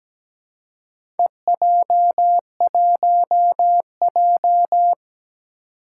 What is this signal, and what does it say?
Morse code, heard as a keyed tone, about 17 wpm, 700 Hz. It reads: EJ1J
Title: Four Challenging Letter-Number Combinations 17wpm